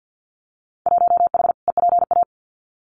Morse code, heard as a keyed tone, about 50 wpm, 700 Hz. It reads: 1H EPA
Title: Winter Field Day Exchanges 50wpm